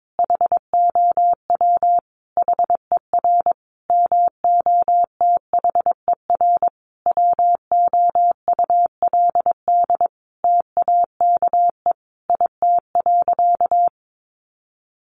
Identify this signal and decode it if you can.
Morse code, heard as a keyed tone, about 22 wpm, 700 Hz. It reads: HOW HER MOTHER WOULD TAKE IT.
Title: Sentences from Top 700 Words 22wpm